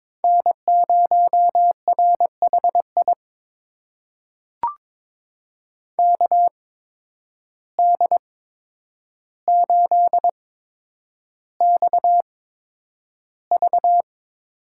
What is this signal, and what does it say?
Morse code, heard as a keyed tone, about 22 wpm, spaced as for 9 wpm, 700 Hz. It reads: N0RHI KD8XV
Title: USA Call Signs - 5 Characters - 8x Character Spacing - 22wpm